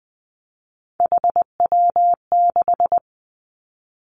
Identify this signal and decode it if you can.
Morse code, heard as a keyed tone, about 20 wpm, 700 Hz. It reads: HW6
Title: Three Challenging Letter-Number Combinations 20wpm